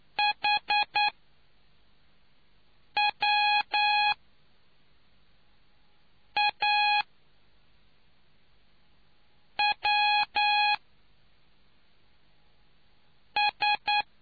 Morse code, heard as a keyed tone, about 9 wpm, spaced as for 3 wpm, 830 Hz. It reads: HWAWS